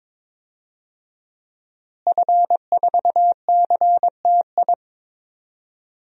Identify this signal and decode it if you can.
Morse code, heard as a keyed tone, about 22 wpm, 700 Hz. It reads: F4CTI
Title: DXCC-100 Call Signs - 5 Characters - Repeated 3x - 22wpm